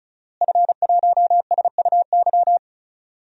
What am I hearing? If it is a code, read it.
Morse code, heard as a keyed tone, about 34 wpm, 700 Hz. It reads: F1SUY
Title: DXCC-100 Call Signs - 5 Characters - Repeated 2x - 35wpm